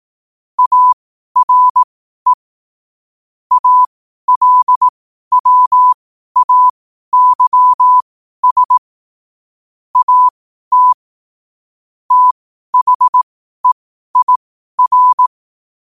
Morse code, heard as a keyed tone, about 18 wpm, spaced as for 12 wpm, 1000 Hz. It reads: ARE ALWAYS AT THEIR
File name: qotdfiles / 12wpm